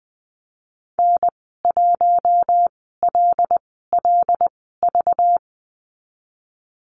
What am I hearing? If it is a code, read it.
Morse code, heard as a keyed tone, about 20 wpm, spaced as for 14 wpm, 700 Hz. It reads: N1LLV
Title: 193 Lesson 31g - Letter v - Call Signs 20wpm